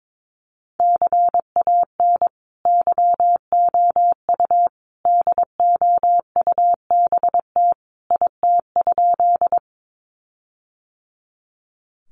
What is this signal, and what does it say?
Morse code, heard as a keyed tone, about 22 wpm, 700 Hz. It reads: CAN YOU DOUBT IT?
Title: Sentences from Top 1500 Words 22wpm